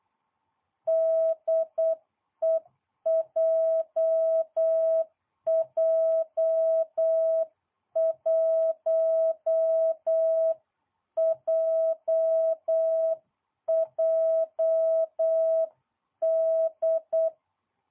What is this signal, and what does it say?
Morse code, heard as a keyed tone, about 8 wpm, 660 Hz. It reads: DE JJ1JJD